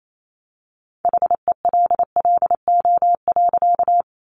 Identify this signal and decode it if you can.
Morse code, heard as a keyed tone, about 28 wpm, 700 Hz. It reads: HELLO.
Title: Sentences from Top 1100 Words 28wpm